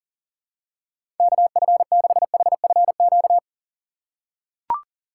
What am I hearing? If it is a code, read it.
Morse code, heard as a keyed tone, about 40 wpm, 700 Hz. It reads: KF6HFQ
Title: USA Call Signs - 6 Characters - 2x Character Spacing - 40wpm